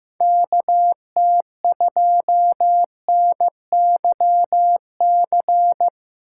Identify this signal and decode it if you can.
Morse code, heard as a keyed tone, about 15 wpm, 700 Hz. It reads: KT2NYC
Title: USA Call Signs - 6 Characters - 2x Character Spacing - 15wpm